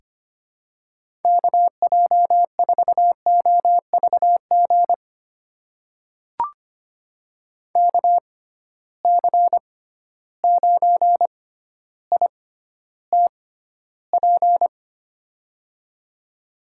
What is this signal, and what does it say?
Morse code, heard as a keyed tone, about 25 wpm, spaced as for 13 wpm, 700 Hz. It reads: KJ4OVG KC9ITP
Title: USA Call Signs - 6 Characters - 6x Character Spacing - 25wpm